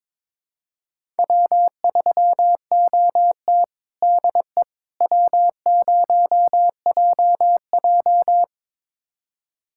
Morse code, heard as a keyed tone, about 21 wpm, 700 Hz. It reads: W3OT DE W0JJ
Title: USA Callsign DE Callsign - 4 Characters - 22wpm